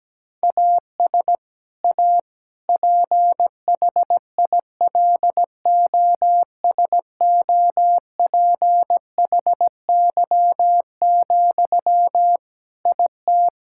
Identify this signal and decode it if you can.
Morse code, heard as a keyed tone, about 17 wpm, 700 Hz. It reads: AS A PHILOSOPHY, IT
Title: Sentences from Top 1000 Words - Review Part 6 - 17wpm